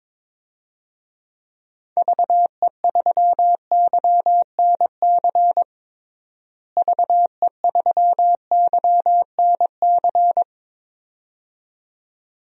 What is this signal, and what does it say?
Morse code, heard as a keyed tone, about 22 wpm, 700 Hz. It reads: VE3YNC VE3YNC
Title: DXCC-100 Call Signs - 6 Characters - Repeated 2x - 22wpm